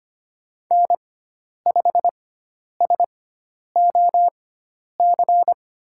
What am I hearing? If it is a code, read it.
Morse code, heard as a keyed tone, about 25 wpm, spaced as for 10 wpm, 700 Hz. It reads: N5SOC
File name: Lesson 16g - Letter c - Call Signs 25wpm